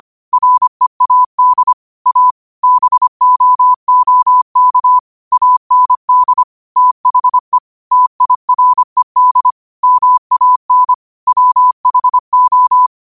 Morse code, heard as a keyed tone, about 25 wpm, 1000 Hz. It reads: READ A BOOK AND THE TIRED MAN WHO
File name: qotdfiles / 25wpm